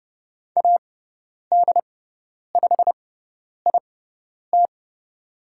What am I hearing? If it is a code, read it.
Morse code, heard as a keyed tone, about 30 wpm, spaced as for 10 wpm, 700 Hz. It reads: AD5IT
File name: Lesson 17g - Letter u - Call Signs 30wpm